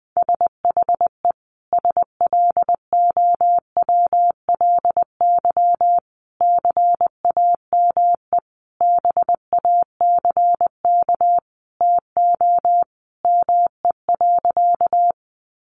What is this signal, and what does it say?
Morse code, heard as a keyed tone, about 20 wpm, 700 Hz. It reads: SHE SLOWLY CAME BACK TO ME.
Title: Sentences from Top 1100 Words 20wpm